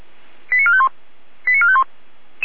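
Five-tone selective call (ZVEI), heard as quick runs of five tones, tones of about 70 ms. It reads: 89451 89451
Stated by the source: ZVEI five-tone call 89452 received on an SDR.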